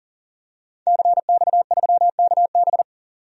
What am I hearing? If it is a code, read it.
Morse code, heard as a keyed tone, about 40 wpm, 700 Hz. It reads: CX3KB
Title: DXCC-100 Call Signs - 5 Characters - 40wpm